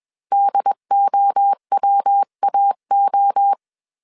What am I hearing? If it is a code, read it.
Morse code, heard as a keyed tone, about 21 wpm, 800 Hz. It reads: DOWAO
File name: CW / DET / 1 3 1500C S B 20wpm 1754